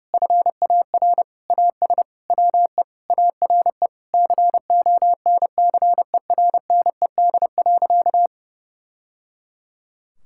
Morse code, heard as a keyed tone, about 30 wpm, 700 Hz. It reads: FAR AS WE ARE CONCERNED.
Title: Sentences from Top 1000 Words - Review Part 6 - 30wpm